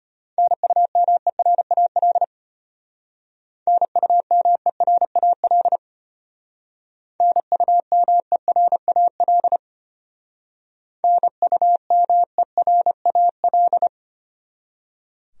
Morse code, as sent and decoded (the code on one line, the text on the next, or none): -. ..- -- . .-. .- .-.. / -. ..- -- . .-. .- .-.. / -. ..- -- . .-. .- .-.. / -. ..- -- . .-. .- .-..
NUMERAL NUMERAL NUMERAL NUMERAL